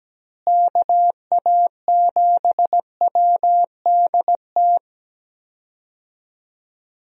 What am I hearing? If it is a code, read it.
Morse code, heard as a keyed tone, about 17 wpm, 700 Hz. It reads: KA7WDT
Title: USA Call Signs - 6 Characters - Repeated 3x - 17wpm